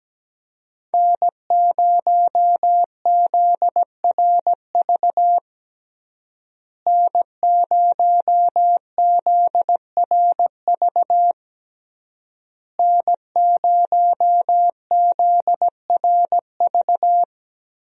Morse code, heard as a keyed tone, about 17 wpm, 700 Hz. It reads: N0ZRV N0ZRV N0ZRV
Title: USA Call Signs - 5 Characters - Repeated 3x - 17wpm